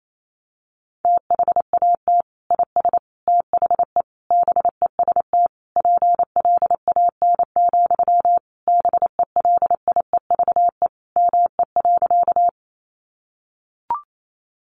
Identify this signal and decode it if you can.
Morse code, heard as a keyed tone, about 28 wpm, 700 Hz. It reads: THAT IS THE BEST PLAN, BELIEVE ME.
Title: Sentences from Top 1000 Words 28wpm